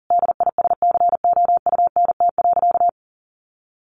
Morse code, heard as a keyed tone, about 40 wpm, 700 Hz. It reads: DISCOUNT.
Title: Sentences from Top 1200 Words 40wpm